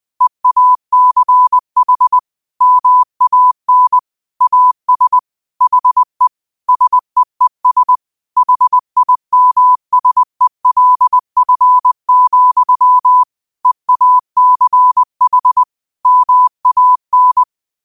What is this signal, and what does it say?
Morse code, heard as a keyed tone, about 20 wpm, 1000 Hz. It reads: EACH MAN AS HE SEES HIMSELF, EACH MAN